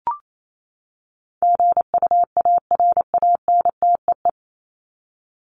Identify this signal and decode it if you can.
Morse code, heard as a keyed tone, about 28 wpm, 700 Hz. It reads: GUARANTEE
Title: Top 1501-1750 Words 28wpm